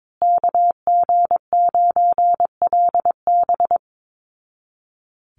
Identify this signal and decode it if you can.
Morse code, heard as a keyed tone, about 22 wpm, 700 Hz. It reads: KG9LB
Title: Call Signs Repeated 2x - 22wpm